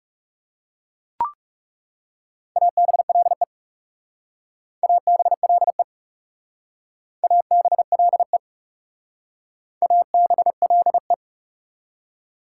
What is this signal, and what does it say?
Morse code, heard as a keyed tone, about 39 wpm, 700 Hz. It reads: ABLE ABLE ABLE ABLE